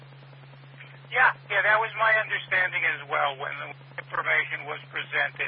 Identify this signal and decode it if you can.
Emergency Alert System (SAME) tones, heard as harsh buzzing bursts, with speech. It is heard from 1.0 to 4.6 s.